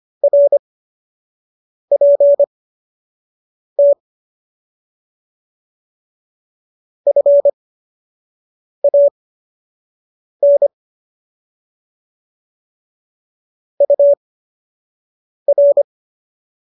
Morse code, heard as a keyed tone, about 25 wpm, spaced as for 6 wpm, 570 Hz. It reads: RPT FAN UR
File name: qso002 06